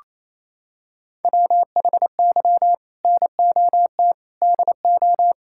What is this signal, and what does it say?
Morse code, heard as a keyed tone, about 28 wpm, 700 Hz. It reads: WHY NOT DO
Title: Sentences from Top 1000 Words - Review Part 6 - 28wpm